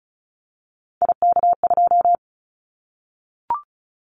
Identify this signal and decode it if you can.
Morse code, heard as a keyed tone, about 34 wpm, 700 Hz. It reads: IK2
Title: Callsign Prefixes for Top 25 DXCC - Repeated 3x - 35wpm